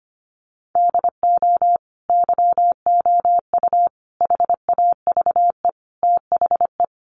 Morse code, heard as a keyed tone, about 25 wpm, 700 Hz. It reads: DO YOU HAVE THE
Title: Sentences from Top 1400 Words 25wpm